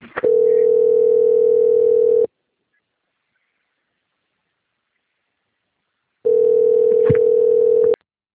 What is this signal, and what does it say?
Ringback tone (North America), 440 + 480 Hz.
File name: out- - - - - .73611